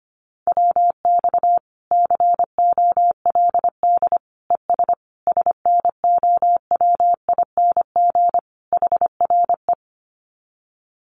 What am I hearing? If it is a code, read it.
Morse code, heard as a keyed tone, about 25 wpm, 700 Hz. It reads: WX COLD ES SNOWING HRE
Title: Common QSO Phrases 25wpm